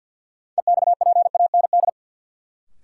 Morse code, heard as a keyed tone, about 47 wpm, 700 Hz. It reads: EXPAND
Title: Sentences from Top 1400 Words - New Word Drill - 50wpm